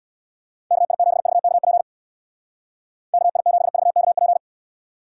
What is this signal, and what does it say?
Morse code, heard as a keyed tone, about 64 wpm, 700 Hz. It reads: KI7VKP KI7VKP